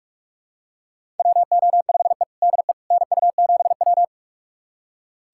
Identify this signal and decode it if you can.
Morse code, heard as a keyed tone, about 42 wpm, 700 Hz. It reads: WO5E DE NU7W
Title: USA Callsign DE Callsign - 4 Characters - 45wpm